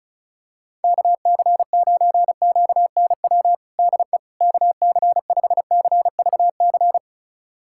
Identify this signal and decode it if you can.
Morse code, heard as a keyed tone, about 36 wpm, 700 Hz. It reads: KC9QNW DE KC5CVC
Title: USA Callsign DE Callsign 35wpm